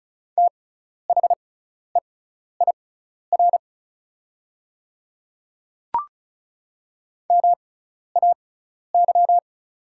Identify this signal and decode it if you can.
Morse code, heard as a keyed tone, about 36 wpm, spaced as for 12 wpm, 690 Hz. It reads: THEIR MAY
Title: 6x Character Spacing - Top 100 Words - 35wpm